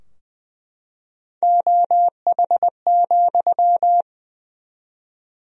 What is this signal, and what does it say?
Morse code, heard as a keyed tone, about 20 wpm, 700 Hz. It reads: OH,
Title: 4x Word-Spacing - Sentences from Top 500 Words - 20wpm